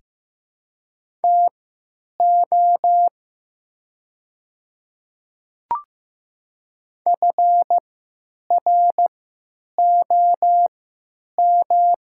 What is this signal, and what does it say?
Morse code, heard as a keyed tone, about 15 wpm, spaced as for 9 wpm, 700 Hz. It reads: TO FROM
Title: 3x Character Spacing - Top 300 Words - 15wpm